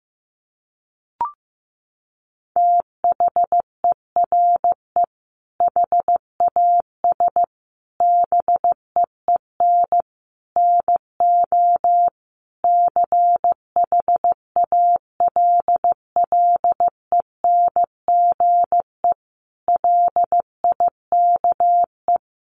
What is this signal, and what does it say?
Morse code, heard as a keyed tone, about 15 wpm, 700 Hz. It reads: THERE HAS BEEN NO CHALLENGE LIKE
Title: Sentences from Top 1500 Words 15wpm